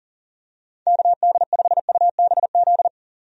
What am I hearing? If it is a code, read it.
Morse code, heard as a keyed tone, about 40 wpm, 700 Hz. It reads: KD5UBZ
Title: USA Call Signs - 6 Characters - 40wpm